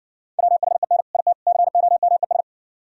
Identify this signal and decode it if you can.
Morse code, heard as a keyed tone, about 60 wpm, 700 Hz. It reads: WHEN IT COMES